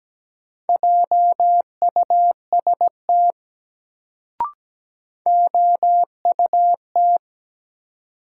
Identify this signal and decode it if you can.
Morse code, heard as a keyed tone, about 17 wpm, 700 Hz. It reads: JUST OUT
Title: Sets of 2 Words - New Word Drill - 17wpm